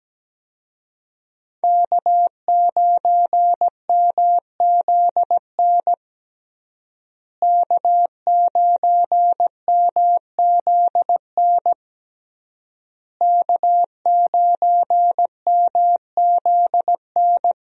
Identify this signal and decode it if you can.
Morse code, heard as a keyed tone, about 17 wpm, 700 Hz. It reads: K9MZN K9MZN K9MZN
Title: USA Call Signs - 5 Characters - Repeated 3x - 17wpm